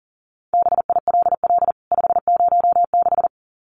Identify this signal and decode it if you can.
Morse code, heard as a keyed tone, about 40 wpm, 700 Hz. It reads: BILL 506
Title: CWT Exchange 40wpm